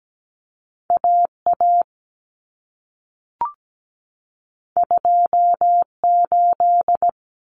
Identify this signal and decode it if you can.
Morse code, heard as a keyed tone, about 17 wpm, 700 Hz. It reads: AA 28
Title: Common Cut Numbers - 2-Digit 17wpm